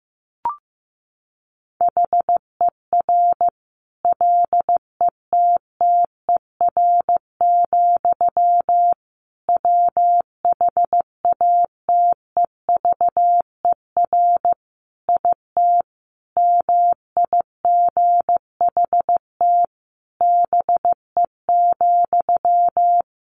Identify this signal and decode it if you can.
Morse code, heard as a keyed tone, about 15 wpm, 700 Hz. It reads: HER LETTER, WHATEVER IT MIGHT BE,
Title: Sentences from Top 1100 Words 15wpm